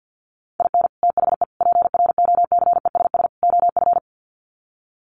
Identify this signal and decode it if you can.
Morse code, heard as a keyed tone, about 50 wpm, 700 Hz. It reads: IN THE PROCESS OF
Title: Sets of 4 Words 50wpm